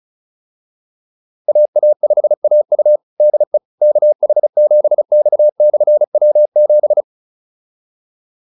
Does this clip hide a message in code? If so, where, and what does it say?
Yes: Morse code from 1.5 to 7.0 s, heard as a keyed tone, about 34 wpm, 600 Hz. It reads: AA5AU DE KH7X/W7